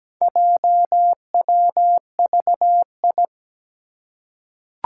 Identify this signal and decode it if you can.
Morse code, heard as a keyed tone, about 17 wpm, 700 Hz. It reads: JWVI
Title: Four Challenging Letter-Number Combinations 17wpm